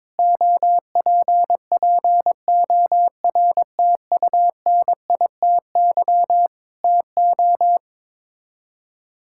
Morse code, heard as a keyed tone, about 22 wpm, 700 Hz. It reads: OPPORTUNITY TO
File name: practice / 22wpm / Sets of 7 Words - 22wpm